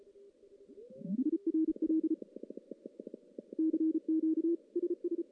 Morse code, heard as a keyed tone, about 36 wpm, 340 Hz. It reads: IRL CQ SS
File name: contests / CW-SS-k5tr- -1600